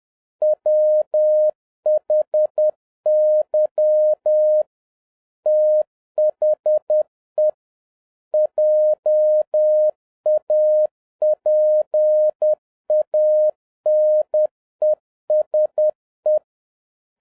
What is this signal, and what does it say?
Morse code, heard as a keyed tone, about 10 wpm, 610 Hz. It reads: WHY THE JAPANESE